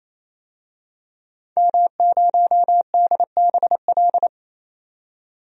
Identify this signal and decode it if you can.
Morse code, heard as a keyed tone, about 28 wpm, 700 Hz. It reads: M0DBL
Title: DXCC-100 Call Signs - 5 Characters - 28wpm